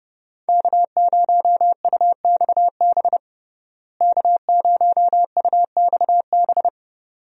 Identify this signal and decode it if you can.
Morse code, heard as a keyed tone, about 30 wpm, 700 Hz. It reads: K0UXB K0UXB
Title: USA Call Signs - 5 Characters - Repeated 2x - 30wpm